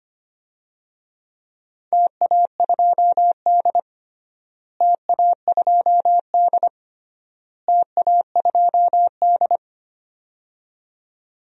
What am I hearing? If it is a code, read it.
Morse code, heard as a keyed tone, about 25 wpm, 700 Hz. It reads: TA2D TA2D TA2D
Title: DXCC-100 Call Signs - 4 Characters - Repeated 3x - 25wpm